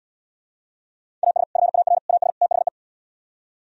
Morse code, heard as a keyed tone, about 78 wpm, 700 Hz. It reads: AS FAR AS THE